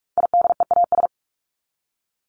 Morse code, heard as a keyed tone, about 44 wpm, 700 Hz. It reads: IDEAS
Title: ES:S Suffix - Top 100 Words with Base Words 45wpm